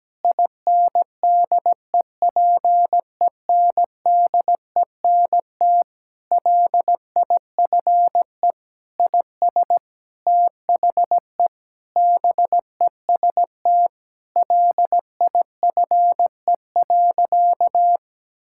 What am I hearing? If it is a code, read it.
Morse code, heard as a keyed tone, about 17 wpm, 700 Hz. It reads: INDEPENDENT LIFE IS THE BEST LIFE.
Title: Sentences from Top 1000 Words - Review Part 6 - 17wpm